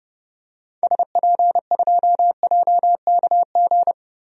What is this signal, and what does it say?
Morse code, heard as a keyed tone, about 30 wpm, 700 Hz. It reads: SP2JKG
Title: DXCC-100 Call Signs - 6 Characters - 30wpm